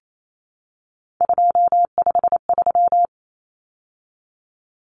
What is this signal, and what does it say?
Morse code, heard as a keyed tone, about 28 wpm, 700 Hz. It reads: 253
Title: Numbers - 3-Digit 28wpm